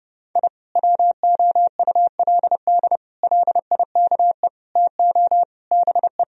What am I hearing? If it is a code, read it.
Morse code, heard as a keyed tone, about 30 wpm, 700 Hz. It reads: I WOULD LIKE TO BE
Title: Sentences from Top 1000 Words - Review Part 3 - 30wpm